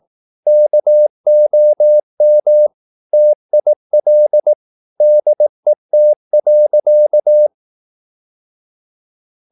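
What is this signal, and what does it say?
Morse code, heard as a keyed tone, about 18 wpm, 600 Hz. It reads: KOM TIL DET.
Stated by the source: Slow (QRS) Morse practice at 18 wpm, CW med Gnister.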